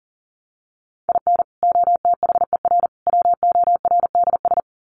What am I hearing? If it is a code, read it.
Morse code, heard as a keyed tone, about 40 wpm, 700 Hz. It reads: IN OTHER WORDS
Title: Sets of 3 Words 40wpm